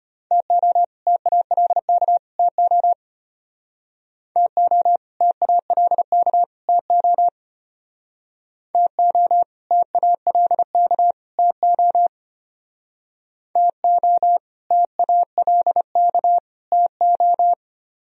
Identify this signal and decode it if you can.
Morse code, heard as a keyed tone, about 31 wpm, 700 Hz. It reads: TO TALK TO TO TALK TO TO TALK TO TO TALK TO
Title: Speed-Racing - Sets of 3 Words 25wpm